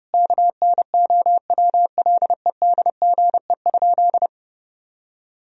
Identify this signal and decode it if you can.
Morse code, heard as a keyed tone, about 32 wpm, 700 Hz. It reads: KNOWLEDGE?
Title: Sentences from Top 1000 Words - Review Part 3 - 30wpm